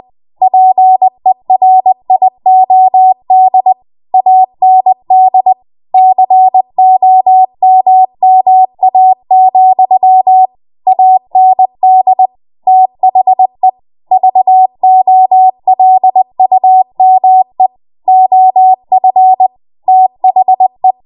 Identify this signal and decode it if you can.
Morse code, heard as a keyed tone, about 20 wpm, 750 Hz. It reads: PERIOD AND COMMA, AND THE VOLUME OF THE
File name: Morse / 20WPM